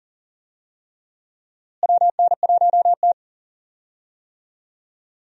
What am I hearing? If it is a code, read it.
Morse code, heard as a keyed tone, about 40 wpm, 700 Hz. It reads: WN1T
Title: USA Call Signs - 4 Characters - Repeated 3x - 40wpm